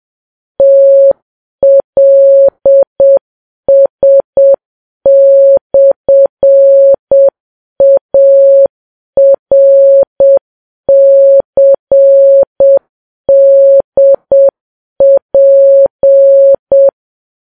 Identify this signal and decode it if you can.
Morse code, heard as a keyed tone, about 7 wpm, 550 Hz. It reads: TLS/ARCDP
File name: Files Web tls / Morse on9fv / morse7kps